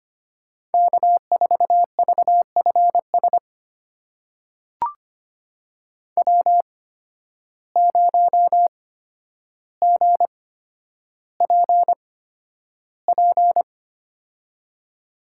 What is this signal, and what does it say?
Morse code, heard as a keyed tone, about 25 wpm, spaced as for 11 wpm, 700 Hz. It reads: K4VFS W0GPP
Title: USA Call Signs - 5 Characters - 8x Character Spacing - 25wpm